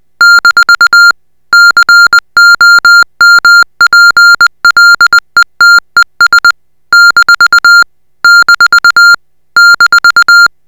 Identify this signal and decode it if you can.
Morse code, heard as a keyed tone, about 20 wpm, 1400 Hz. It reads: - COMPLETES - - -